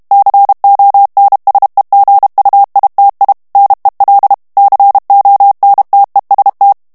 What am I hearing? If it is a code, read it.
Morse code, heard as a keyed tone, about 32 wpm, 800 Hz. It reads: CONSEGUITI NEL CONTEST
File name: texte clair 2 vitesse 30 mots minute